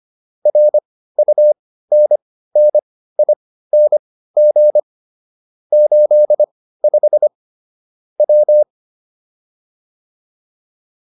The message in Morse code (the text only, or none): RUNNING 85 W